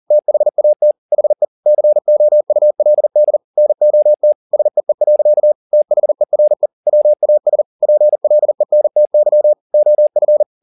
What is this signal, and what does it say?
Morse code, heard as a keyed tone, about 40 wpm, 600 Hz. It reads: THAT HE COULD NOT SEE. THERE WAS PLENTY OF
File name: the old man and the sea 2 40wpm